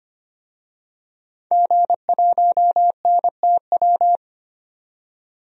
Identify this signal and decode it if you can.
Morse code, heard as a keyed tone, about 25 wpm, 700 Hz. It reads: G1NTW
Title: DXCC-100 Call Signs - 5 Characters - 25wpm